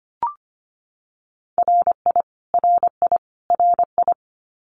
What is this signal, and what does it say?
Morse code, heard as a keyed tone, about 25 wpm, 700 Hz. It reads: RI RI RI
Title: US States Repeated 3x 25wpm